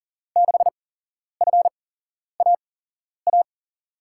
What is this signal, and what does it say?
Morse code, heard as a keyed tone, about 39 wpm, spaced as for 11 wpm, 700 Hz. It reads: 6FAA
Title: USA Call Signs - 6 Characters - 8x Character Spacing - 40wpm